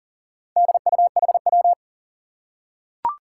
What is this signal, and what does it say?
Morse code, heard as a keyed tone, about 40 wpm, 700 Hz. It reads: DUHW
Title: Four Challenging Letter-Number Combinations 40wpm